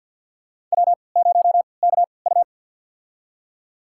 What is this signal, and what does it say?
Morse code, heard as a keyed tone, about 51 wpm, spaced as for 28 wpm, 700 Hz. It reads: W0KU
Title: USA Call Signs - 4 Characters - 3x Character Spacing - 50wpm